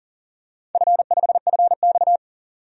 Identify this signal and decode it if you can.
Morse code, heard as a keyed tone, about 40 wpm, 700 Hz. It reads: F5FX
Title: DXCC-100 Call Signs - 4 Characters - 40wpm